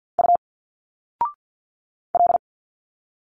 Morse code, heard as a keyed tone, about 51 wpm, 700 Hz. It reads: U L
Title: Single Letters 50wpm